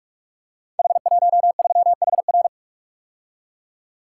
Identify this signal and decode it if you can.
Morse code, heard as a keyed tone, about 44 wpm, 700 Hz. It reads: H13HR
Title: Code Groups - 5 Letter:Numbers - 45wpm